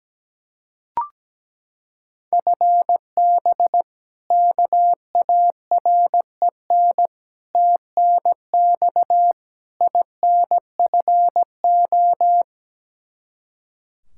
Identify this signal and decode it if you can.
Morse code, heard as a keyed tone, about 17 wpm, 700 Hz. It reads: FB KAREN TNX INFO
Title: K1USN SST QSO Practice 17wpm